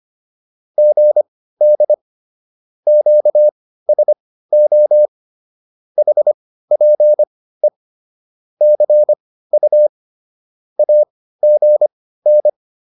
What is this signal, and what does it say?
Morse code, heard as a keyed tone, about 25 wpm, spaced as for 15 wpm, 600 Hz. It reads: GD QSO HPE CU AGN